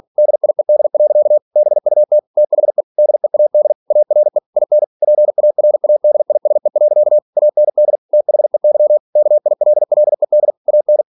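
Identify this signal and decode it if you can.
Morse code, heard as a keyed tone, about 46 wpm, 600 Hz. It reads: DIED. BUT THE DEAD ARE IN PARADISE. AND THEY KILLED AN
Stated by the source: Practice text from an English novel sent at 47 WPM.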